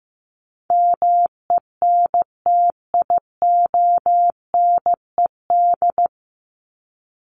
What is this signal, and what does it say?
Morse code, heard as a keyed tone, about 15 wpm, 700 Hz. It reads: MENTIONED
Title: ED Suffix - Top 100 Words with Base Words 15wpm